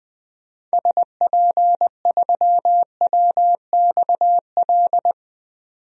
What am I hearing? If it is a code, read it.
Morse code, heard as a keyed tone, about 20 wpm, 700 Hz. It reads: SP3WXL
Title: DXCC-100 Call Signs - 6 Characters - 20wpm